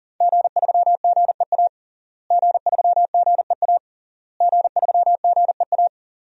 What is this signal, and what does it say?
Morse code, heard as a keyed tone, about 40 wpm, 700 Hz. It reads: G3GEA G3GEA G3GEA
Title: DXCC-100 Call Signs - 5 Characters - Repeated 3x - 40wpm